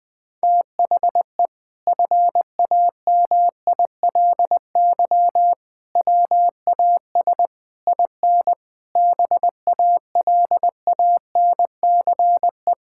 Morse code, heard as a keyed tone, about 20 wpm, 700 Hz. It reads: THE FAMILY WAS IN BALANCE
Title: Sentences from Top 1000 Words - Review Part 6 - 20wpm